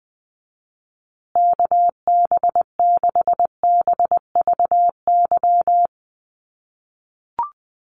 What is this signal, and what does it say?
Morse code, heard as a keyed tone, about 19 wpm, 700 Hz. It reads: KB6BVY
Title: USA Call Signs Repeated 2x 20wpm